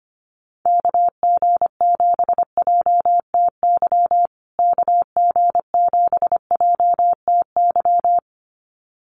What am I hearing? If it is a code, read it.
Morse code, heard as a keyed tone, about 25 wpm, 700 Hz. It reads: KG7JTY KG7JTY
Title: USA Call Signs Repeated 2x 25wpm